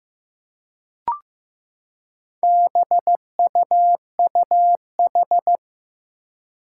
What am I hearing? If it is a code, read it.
Morse code, heard as a keyed tone, about 15 wpm, 700 Hz. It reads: BUUH
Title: Four Challenging Letter-Number Combinations 15wpm